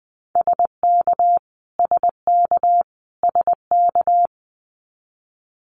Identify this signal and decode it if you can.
Morse code, heard as a keyed tone, about 20 wpm, 700 Hz. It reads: SK SK SK
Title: Canadian Provinces and Territories - Repeated 3x - 20wpm